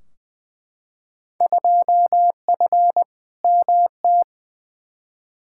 Morse code, heard as a keyed tone, about 20 wpm, 700 Hz. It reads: 2F MT
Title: Field Day Exchanges Encore - 20wpm